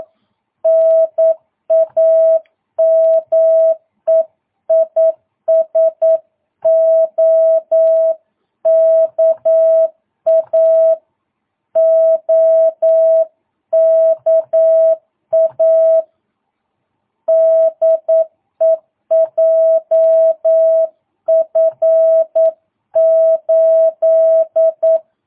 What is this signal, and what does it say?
Morse code, heard as a keyed tone, about 9 wpm, 650 Hz. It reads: NAME IS OKA OKA DE JF8
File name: jh8cbh / 2022cw